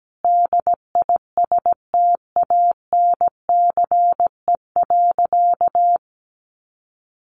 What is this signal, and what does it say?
Morse code, heard as a keyed tone, about 17 wpm, 700 Hz. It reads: DISTANCE.
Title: Sentences from Top 1200 Words 17wpm